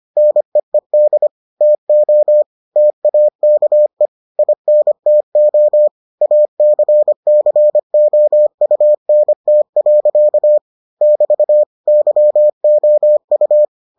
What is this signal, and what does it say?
Morse code, heard as a keyed tone, about 25 wpm, 600 Hz. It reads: NEED TO TAKE INTO ACCOUNT. <BT> YOU